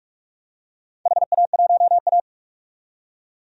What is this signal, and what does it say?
Morse code, heard as a keyed tone, about 47 wpm, 700 Hz. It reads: HA1A
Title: DXCC-100 Call Signs - 4 Characters - 45wpm